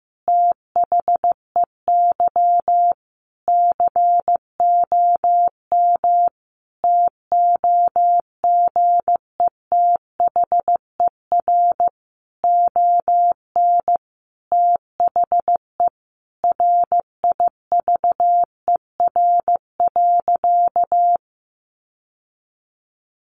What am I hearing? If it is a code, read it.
Morse code, heard as a keyed tone, about 15 wpm, 700 Hz. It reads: THEY COM TOGETHER ON THE RIVER.
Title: Sentences from Top 1300 Words 15wpm